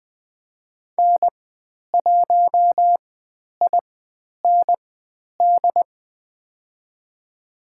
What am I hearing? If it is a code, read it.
Morse code, heard as a keyed tone, about 20 wpm, spaced as for 10 wpm, 700 Hz. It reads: N1IND
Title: 095 Lesson 16g - Letter c - Call Signs 20wpm